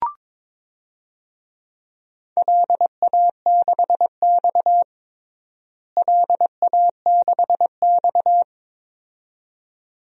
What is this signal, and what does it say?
Morse code, heard as a keyed tone, about 22 wpm, 700 Hz. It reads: LA6X LA6X
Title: DXCC-100 Call Signs - 4 Characters - Repeated 2x - 22wpm